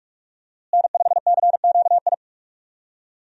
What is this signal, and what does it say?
Morse code, heard as a keyed tone, about 46 wpm, 700 Hz. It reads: N5CQI
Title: USA Call Signs - 5 Characters - Repeated 3x - 45wpm